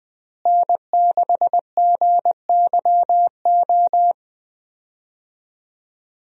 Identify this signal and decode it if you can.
Morse code, heard as a keyed tone, about 21 wpm, 700 Hz. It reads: N6GYO
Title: USA Call Signs - 5 Characters - Repeated 3x - 20wpm